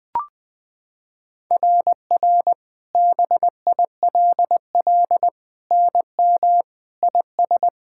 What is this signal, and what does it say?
Morse code, heard as a keyed tone, about 20 wpm, 700 Hz. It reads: RR BILL NM IS
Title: SOTA QSO Practice 20wpm